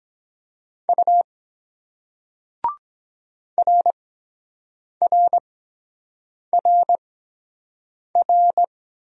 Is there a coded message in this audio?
Yes: Morse code from 0.9 to 8.7 s, heard as a keyed tone, about 23 wpm, spaced as for 7 wpm, 700 Hz. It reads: U RRRR